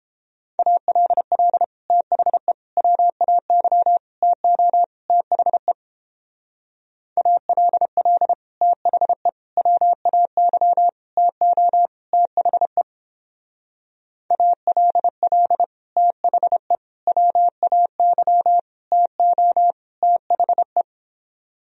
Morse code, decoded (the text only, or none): ALL THE WAY TO THE ALL THE WAY TO THE ALL THE WAY TO THE